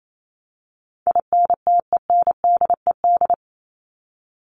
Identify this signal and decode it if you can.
Morse code, heard as a keyed tone, about 28 wpm, 700 Hz. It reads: INTENDED
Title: ED Suffix - Top 100 Words with Base Words 28wpm